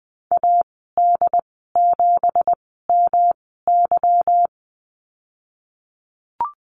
Morse code, heard as a keyed tone, about 20 wpm, 700 Hz. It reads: A D 7 M Y
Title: 207 Lesson 33g - Letter j - Call Signs 20wpm